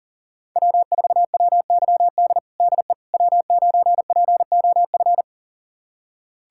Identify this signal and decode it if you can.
Morse code, heard as a keyed tone, about 40 wpm, 700 Hz. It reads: W4WYD DE W9POF
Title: USA Callsign DE Callsign - 5 Characters - 40wpm